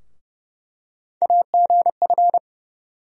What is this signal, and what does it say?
Morse code, heard as a keyed tone, about 31 wpm, 700 Hz. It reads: AGF
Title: USA Call Sign Postfixes - 3 Letters 30wpm